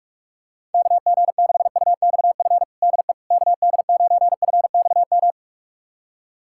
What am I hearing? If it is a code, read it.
Morse code, heard as a keyed tone, about 43 wpm, 700 Hz. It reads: KG6UXF DE KD9FXM
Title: USA Callsign DE Callsign 45wpm